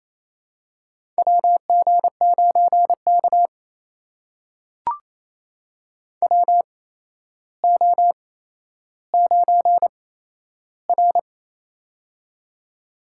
Morse code, heard as a keyed tone, about 28 wpm, spaced as for 12 wpm, 700 Hz. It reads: WG9K WO9R
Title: USA Call Signs - 4 Characters - 8x Character Spacing - 28wpm